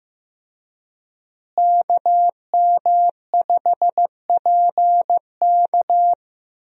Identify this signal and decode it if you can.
Morse code, heard as a keyed tone, about 15 wpm, 700 Hz. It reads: KM5PK
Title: USA Call Signs - 5 Characters - Repeated 3x - 15wpm